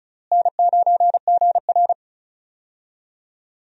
Morse code, heard as a keyed tone, about 36 wpm, 700 Hz. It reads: N9GR
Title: USA Call Signs - 4 Characters - Repeated 3x - 35wpm